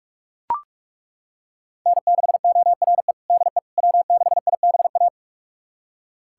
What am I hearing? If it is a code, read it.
Morse code, heard as a keyed tone, about 41 wpm, 700 Hz. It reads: N6ORE DE W6IBA
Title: USA Callsign DE Callsign - 5 Characters - 45wpm